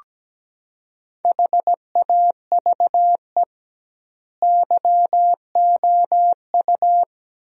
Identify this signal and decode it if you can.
Morse code, heard as a keyed tone, about 17 wpm, 700 Hz. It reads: HAVE YOU
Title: 2x Word-Spacing - Sentences from Top 200 Words - 17wpm